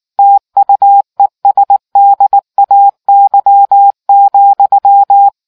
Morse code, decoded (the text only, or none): TUESDAY,